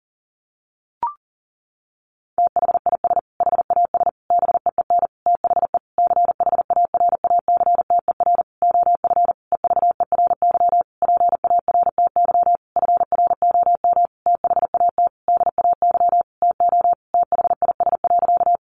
Morse code, heard as a keyed tone, about 40 wpm, 700 Hz. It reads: THIS HAS BEEN THE CHARACTER OF EVERY PARTY FROM THAT DAY TO THIS.
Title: Sentences from Top 1400 Words 40wpm